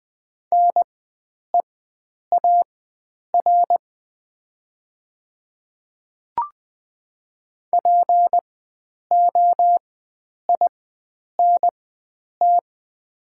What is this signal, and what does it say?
Morse code, heard as a keyed tone, about 20 wpm, spaced as for 9 wpm, 700 Hz. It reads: NEAR POINT
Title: 4x Character Spacing - Top 300 Words - 20wpm